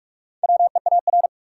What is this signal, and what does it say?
Morse code, heard as a keyed tone, about 46 wpm, 700 Hz. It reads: WEAR
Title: Four Letter Words - 45wpm